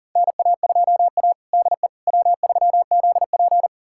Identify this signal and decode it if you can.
Morse code, heard as a keyed tone, about 40 wpm, 700 Hz. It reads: NA2A DE W3ZP
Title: USA Callsign DE Callsign - 4 Characters - 40wpm